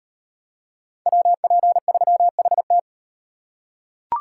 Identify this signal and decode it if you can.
Morse code, heard as a keyed tone, about 38 wpm, 700 Hz. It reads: WP3HT